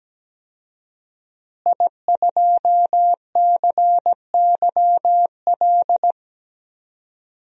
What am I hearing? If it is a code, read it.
Morse code, heard as a keyed tone, about 17 wpm, 700 Hz. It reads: I2CYL
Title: DXCC-100 Call Signs - 5 Characters - Repeated 3x - 17wpm